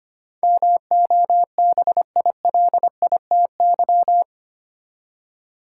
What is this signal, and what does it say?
Morse code, heard as a keyed tone, about 25 wpm, 700 Hz. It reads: MOBILITY
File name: Eight Letter Words - 25wpm